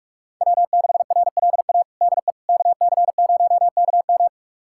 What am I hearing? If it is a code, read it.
Morse code, heard as a keyed tone, about 41 wpm, 700 Hz. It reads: W6RLA DE KC0KM
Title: USA Callsign DE Callsign - 5 Characters - 45wpm